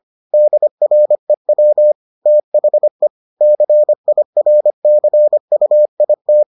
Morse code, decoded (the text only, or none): DREW THE CIRCUIT